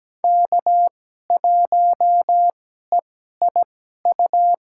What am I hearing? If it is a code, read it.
Morse code, heard as a keyed tone, about 17 wpm, spaced as for 12 wpm, 700 Hz. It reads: K1EIU
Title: USA Call Signs - 5 Characters - 2x Character Spacing - 17wpm